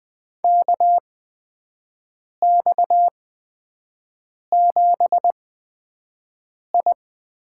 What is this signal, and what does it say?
Morse code, heard as a keyed tone, about 20 wpm, spaced as for 5 wpm, 700 Hz. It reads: KX7I
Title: USA Call Signs - 4 Characters - 8x Character Spacing - 20wpm